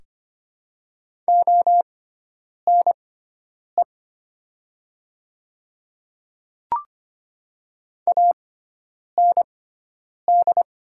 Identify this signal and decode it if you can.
Morse code, heard as a keyed tone, about 25 wpm, spaced as for 9 wpm, 700 Hz. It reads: ONE AND
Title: 6x Character Spacing - Top 100 Words - 25wpm